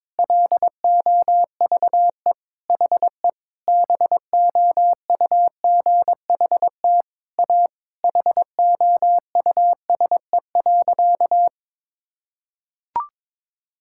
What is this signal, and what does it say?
Morse code, heard as a keyed tone, about 21 wpm, 700 Hz. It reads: LOVE HE BOUGHT A HOUSE.
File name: practice / 22wpm / Sentences with 4-Word Phrases - 22wpm